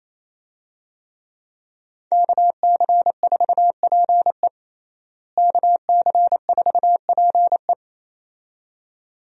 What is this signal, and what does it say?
Morse code, heard as a keyed tone, about 28 wpm, 700 Hz. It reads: KC4PE KC4PE
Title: USA Call Signs - 5 Characters - Repeated 2x - 28wpm